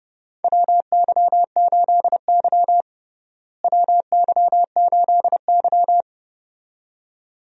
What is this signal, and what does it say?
Morse code, heard as a keyed tone, about 30 wpm, 700 Hz. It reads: WY8Y WY8Y
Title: USA Call Signs - 4 Characters - Repeated 2x - 30wpm